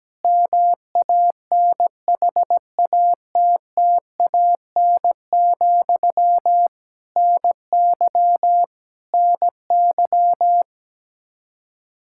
Morse code, heard as a keyed tone, about 17 wpm, 700 Hz. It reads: MANHATTAN, NY NY
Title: Practice QSO - Exchange 1 - Beginning - 17wpm